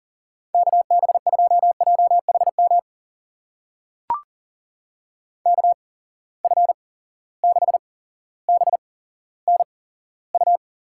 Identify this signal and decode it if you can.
Morse code, heard as a keyed tone, about 41 wpm, spaced as for 17 wpm, 700 Hz. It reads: KB2JHM KF6BNU
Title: USA Call Signs - 6 Characters - 8x Character Spacing - 40wpm